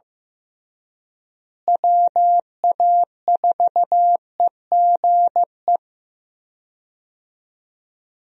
Morse code, heard as a keyed tone, about 15 wpm, 700 Hz. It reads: WA4EGE
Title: USA Call Signs - 6 Characters - Repeated 3x - 15wpm